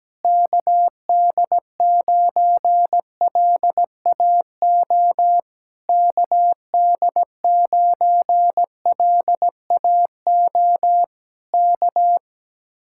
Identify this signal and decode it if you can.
Morse code, heard as a keyed tone, about 17 wpm, 700 Hz. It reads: KD9LAO KD9LAO K
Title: Speed-Racing Common QSO Phrases Encore - 17wpm